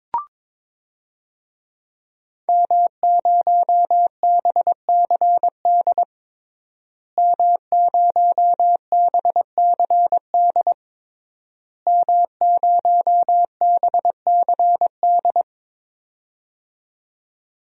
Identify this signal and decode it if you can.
Morse code, heard as a keyed tone, about 21 wpm, 700 Hz. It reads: M0BCD M0BCD M0BCD
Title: DXCC-100 Call Signs - 5 Characters - Repeated 3x - 22wpm